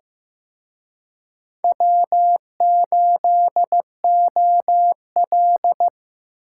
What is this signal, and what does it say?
Morse code, heard as a keyed tone, about 18 wpm, 700 Hz. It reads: W8OL
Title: Speed Racing - USA Call Signs 15wpm